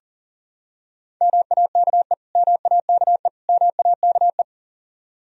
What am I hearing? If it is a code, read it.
Morse code, heard as a keyed tone, about 40 wpm, 700 Hz. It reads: MAKE MAKE MAKE
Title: Top 100 Words Repeated 40wpm